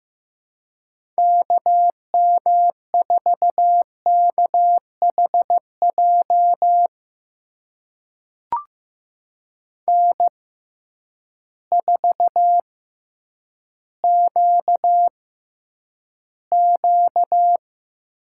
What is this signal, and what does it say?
Morse code, heard as a keyed tone, about 15 wpm, spaced as for 9 wpm, 700 Hz. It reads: KM4KHJ N4QQ
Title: USA Call Signs - 6x Character Spacing - 15wpm